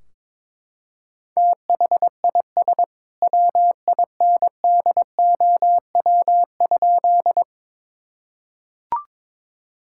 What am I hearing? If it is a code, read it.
Morse code, heard as a keyed tone, about 22 wpm, 700 Hz. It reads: THIS WINDOW?
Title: Sentences from Top 1600 Words 22wpm